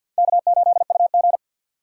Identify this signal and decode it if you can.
Morse code, heard as a keyed tone, about 66 wpm, spaced as for 57 wpm, 700 Hz. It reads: K8UG
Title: Speed Racing - USA 1x2 Call Signs 50wpm